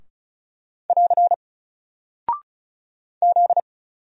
Morse code, heard as a keyed tone, about 36 wpm, 700 Hz. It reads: <AR> Z
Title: Single Character (Letter:Number:Punctuation:Prosign) Alt 35wpm